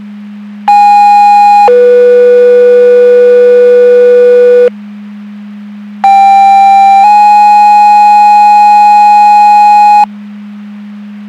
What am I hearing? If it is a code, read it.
Two-tone page, heard as a one-second tone, then a three-second tone, about 820 then 490 Hz, about 790 then 830 Hz. It is heard from 0.7 to 10.0 s.
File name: Portage County 154.3850 Fire EMS Paging